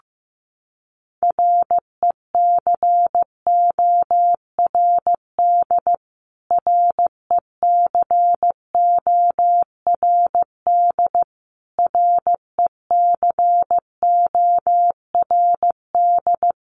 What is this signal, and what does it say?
Morse code, heard as a keyed tone, about 14 wpm, 700 Hz. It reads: RECORD RECORD RECORD
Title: Top 500 Words Repeated 15wpm